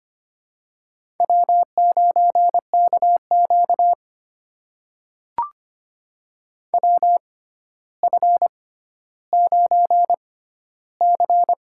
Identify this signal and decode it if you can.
Morse code, heard as a keyed tone, about 25 wpm, spaced as for 13 wpm, 700 Hz. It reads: W9KQ WF9C
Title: USA Call Signs - 4 Characters - 6x Character Spacing - 25wpm